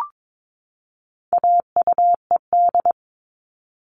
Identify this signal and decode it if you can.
Morse code, heard as a keyed tone, about 22 wpm, 700 Hz. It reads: AUED
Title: All Cut Numbers - 4-Digit 22wpm